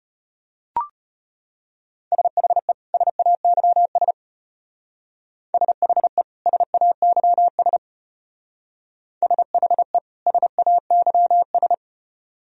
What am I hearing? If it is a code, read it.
Morse code, heard as a keyed tone, about 34 wpm, 700 Hz. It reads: SHE SAYS SHE SAYS SHE SAYS